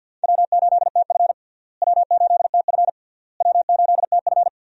Morse code, heard as a keyed tone, about 47 wpm, 700 Hz. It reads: W8TF W8TF W8TF
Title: USA Call Signs - 4 Characters - Repeated 3x - 50wpm